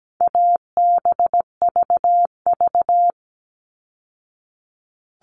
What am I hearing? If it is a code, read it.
Morse code, heard as a keyed tone, about 17 wpm, 700 Hz. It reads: ABVV
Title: All Cut Numbers - 4-Digit 17wpm